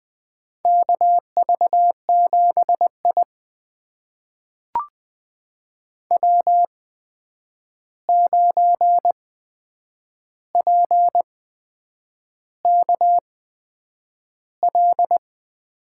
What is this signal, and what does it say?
Morse code, heard as a keyed tone, about 20 wpm, spaced as for 8 wpm, 700 Hz. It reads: KV7I W9PKL
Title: USA Call Signs - 8x Character Spacing - 20wpm